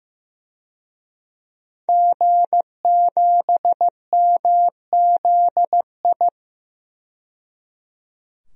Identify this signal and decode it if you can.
Morse code, heard as a keyed tone, about 15 wpm, 700 Hz. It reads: G7MZI
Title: DXCC-100 Call Signs - 5 Characters - 15wpm